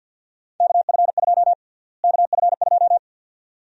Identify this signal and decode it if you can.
Morse code, heard as a keyed tone, about 47 wpm, 700 Hz. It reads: KF2 KF2
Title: USA Call Sign Prefixes - Repeated 2x - 50wpm